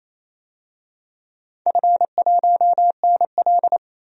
Morse code, heard as a keyed tone, about 28 wpm, 700 Hz. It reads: F1NL
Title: DXCC-100 Call Signs - 4 Characters - 28wpm